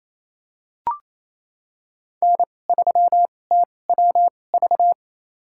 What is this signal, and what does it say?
Morse code, heard as a keyed tone, about 28 wpm, spaced as for 20 wpm, 700 Hz. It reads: N3TWV
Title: USA Call Signs - 5 Characters - 2x Character Spacing - 28wpm